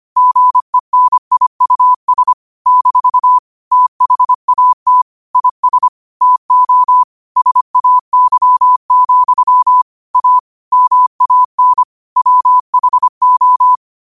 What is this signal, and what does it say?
Morse code, heard as a keyed tone, about 25 wpm, 1000 Hz. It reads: GENIUS - THAT IS TO SAY, A MAN WHO
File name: qotdfiles / 25wpm